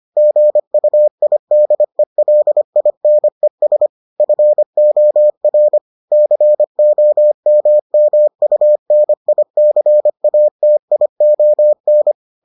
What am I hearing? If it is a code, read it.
Morse code, heard as a keyed tone, about 25 wpm, 600 Hz. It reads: GUIDELINES FOR COMMUNICATION